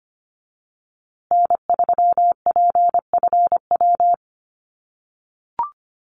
Morse code, heard as a keyed tone, about 25 wpm, 700 Hz. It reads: N3PFW
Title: USA Call Signs Repeated 3x 25wpm